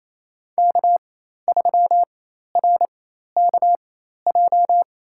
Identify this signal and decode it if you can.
Morse code, heard as a keyed tone, about 28 wpm, spaced as for 13 wpm, 700 Hz. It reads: K3RKJ
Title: USA Call Signs - 5 Characters - 4x Character Spacing - 28wpm